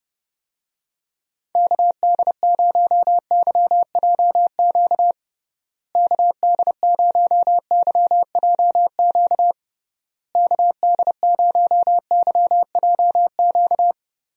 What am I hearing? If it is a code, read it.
Morse code, heard as a keyed tone, about 30 wpm, 700 Hz. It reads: KD0YJQ KD0YJQ KD0YJQ
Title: USA Call Signs - 6 Characters - Repeated 3x - 30wpm